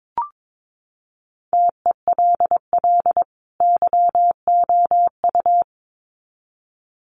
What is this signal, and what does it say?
Morse code, heard as a keyed tone, about 22 wpm, 700 Hz. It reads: TELL YOU
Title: Sets of 2 Words 22wpm